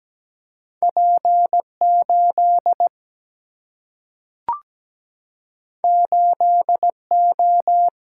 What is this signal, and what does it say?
Morse code, heard as a keyed tone, about 17 wpm, 700 Hz. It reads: P8 8O
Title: Single-Letter Single-Number 17wpm